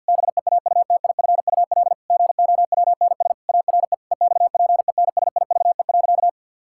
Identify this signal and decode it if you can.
Morse code, heard as a keyed tone, about 50 wpm, 700 Hz. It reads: BEAUTIFUL GOWNS ARE EXPENSIVE.
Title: Pangram Challenge v2 - 50wpm